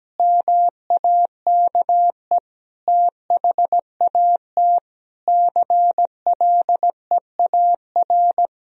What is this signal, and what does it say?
Morse code, heard as a keyed tone, about 17 wpm, 700 Hz. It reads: MAKE THAT CLEAR
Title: Sentences from Top 1000 Words - Review Part 1 - 17wpm